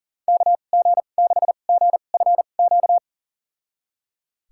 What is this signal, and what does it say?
Morse code, heard as a keyed tone, about 37 wpm, spaced as for 28 wpm, 700 Hz. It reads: KG6GFQ
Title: USA Call Signs - 6 Characters - 2x Character Spacing - 40wpm